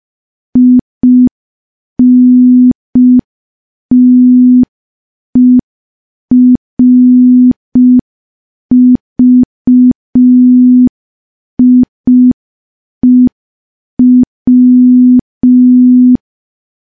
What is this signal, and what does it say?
Morse code, heard as a keyed tone, about 5 wpm, 250 Hz. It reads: INTERVIEW
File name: morse / morse-training-05WPM